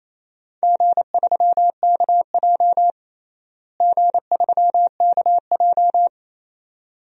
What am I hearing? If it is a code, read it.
Morse code, heard as a keyed tone, about 28 wpm, 700 Hz. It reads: G3KJ G3KJ
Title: DXCC-100 Call Signs - 4 Characters - Repeated 2x - 28wpm